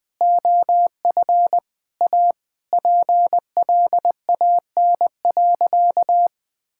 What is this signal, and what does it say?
Morse code, heard as a keyed tone, about 21 wpm, 700 Hz. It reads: OF A PLAN.
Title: Sentences with 3-Word Phrases - 20wpm